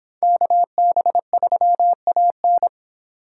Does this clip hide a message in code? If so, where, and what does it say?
Yes: Morse code from 0.2 to 2.7 s, heard as a keyed tone, about 26 wpm, 700 Hz. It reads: KB3AN